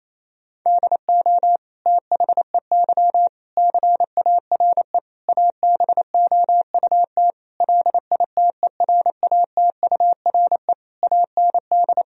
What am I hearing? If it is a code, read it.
Morse code, heard as a keyed tone, about 28 wpm, 700 Hz. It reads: DO THEY CARE ABOUT LITERATURE AND
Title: Sentences from Top 1000 Words - Review Part 3 - 28wpm